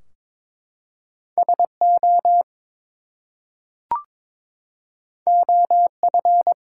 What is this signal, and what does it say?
Morse code, heard as a keyed tone, about 22 wpm, 700 Hz. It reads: SO OF
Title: Two Letter Words - 22wpm